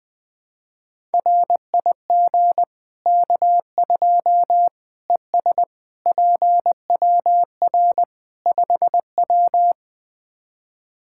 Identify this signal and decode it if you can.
Morse code, heard as a keyed tone, about 20 wpm, 700 Hz. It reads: RIG K2 ES PWR 5W
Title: Practice QSO - Exchange 2 - Info - 20wpm